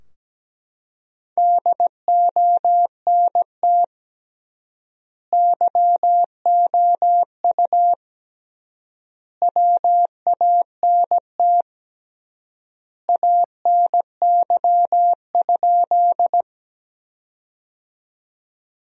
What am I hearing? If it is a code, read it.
Morse code, heard as a keyed tone, about 17 wpm, 700 Hz. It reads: DONT YOU WANT ANY?
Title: 3x Word-Spacing - Sentences from Top 300 Words - 17wpm